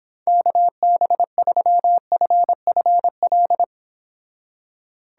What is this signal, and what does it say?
Morse code, heard as a keyed tone, about 31 wpm, 700 Hz. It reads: KB3FFL